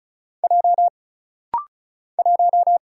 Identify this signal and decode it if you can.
Morse code, heard as a keyed tone, about 36 wpm, 700 Hz. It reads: J 1
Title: Single Letter-Number - Rapid-Fire 35wpm